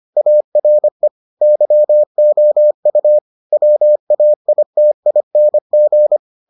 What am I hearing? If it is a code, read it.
Morse code, heard as a keyed tone, about 25 wpm, 600 Hz. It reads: ARE YOU WAITING